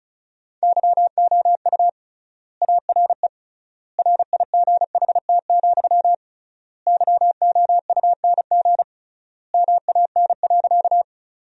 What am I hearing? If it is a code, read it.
Morse code, heard as a keyed tone, about 34 wpm, 700 Hz. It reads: YOU ARE RIGHT, YOUNG MAN.
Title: 3x Word-Spacing - Sentences from Top 400 Words - 35wpm